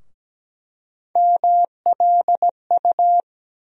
Morse code, heard as a keyed tone, about 17 wpm, 700 Hz. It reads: MLU
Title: USA Call Sign Postfixes - 3 Letters 17wpm